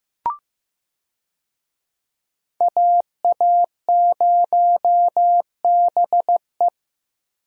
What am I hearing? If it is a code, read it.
Morse code, heard as a keyed tone, about 15 wpm, 700 Hz. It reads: AA0BE
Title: USA Call Signs - 5 Characters - Repeated 3x - 15wpm